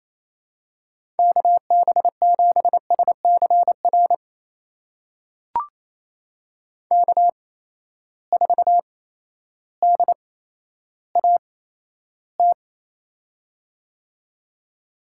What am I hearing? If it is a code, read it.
Morse code, heard as a keyed tone, about 28 wpm, spaced as for 13 wpm, 700 Hz. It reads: KB7SCR K4DAT
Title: USA Call Signs - 8x Character Spacing - 28wpm